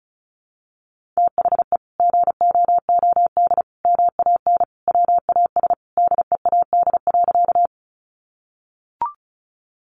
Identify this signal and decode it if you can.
Morse code, heard as a keyed tone, about 34 wpm, 700 Hz. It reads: THE GOOD MAN WAS DEAD.
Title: Sentences from Top 1500 Words 35wpm